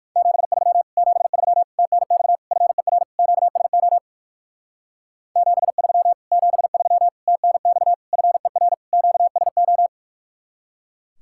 Morse code, heard as a keyed tone, about 55 wpm, 700 Hz. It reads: 73 73 TNX FER QSO 73 73 TNX FER QSO
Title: Speed-Racing Common QSO Phrases - 45wpm